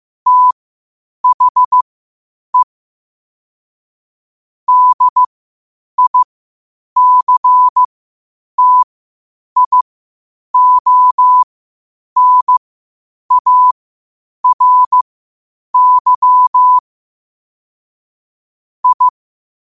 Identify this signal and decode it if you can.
Morse code, heard as a keyed tone, about 15 wpm, spaced as for 8 wpm, 1000 Hz. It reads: THE DICTIONARY I
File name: qotdfiles / 8wpm